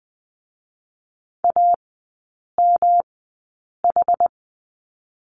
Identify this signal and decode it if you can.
Morse code, heard as a keyed tone, about 20 wpm, spaced as for 8 wpm, 700 Hz. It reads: AMH
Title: 121 Lesson 20e - Number 3 - Three Characters 20wpm